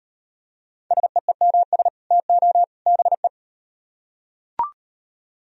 Morse code, heard as a keyed tone, about 38 wpm, 700 Hz. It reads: SEEMS TO BE